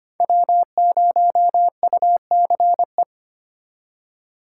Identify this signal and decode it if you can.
Morse code, heard as a keyed tone, about 25 wpm, 700 Hz. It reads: W0UCE
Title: K1USN SST QSO Practice - 25wpm